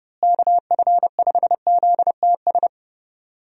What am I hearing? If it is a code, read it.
Morse code, heard as a keyed tone, about 30 wpm, 700 Hz. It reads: KF5ZTS
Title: USA Call Signs - 6 Characters - Repeated 3x - 30wpm